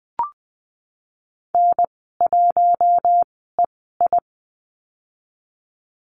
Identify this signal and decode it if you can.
Morse code, heard as a keyed tone, about 20 wpm, spaced as for 15 wpm, 700 Hz. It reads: N1EI
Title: 062 Lesson 11g - Letter h - Call Signs 20wpm